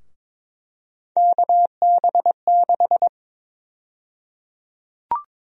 Morse code, heard as a keyed tone, about 22 wpm, 700 Hz. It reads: KB6
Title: USA Call Sign Prefixes - Repeated 2x - 22wpm